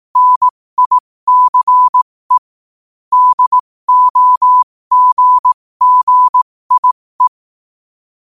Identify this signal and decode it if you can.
Morse code, heard as a keyed tone, about 18 wpm, spaced as for 15 wpm, 1000 Hz. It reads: NICE DOGGIE
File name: qotdfiles / 15wpm